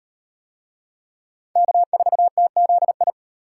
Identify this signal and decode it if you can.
Morse code, heard as a keyed tone, about 34 wpm, 700 Hz. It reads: K4TZI